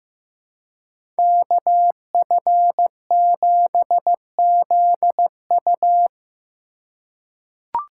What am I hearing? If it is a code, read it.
Morse code, heard as a keyed tone, about 15 wpm, 700 Hz. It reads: KF7ZU
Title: USA Call Signs - 5 Characters - 2x Character Spacing - 15wpm